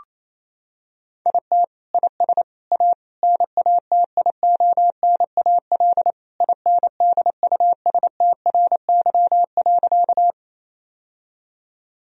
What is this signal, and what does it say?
Morse code, heard as a keyed tone, about 28 wpm, 700 Hz. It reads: IT IS A NATIONAL INDUSTRY.
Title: Sentences from Top 1000 Words - Review Part 4 - 28wpm